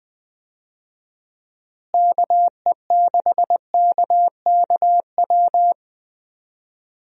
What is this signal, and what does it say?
Morse code, heard as a keyed tone, about 21 wpm, 700 Hz. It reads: KE6KKW
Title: USA Call Signs - 6 Characters - Repeated 3x - 20wpm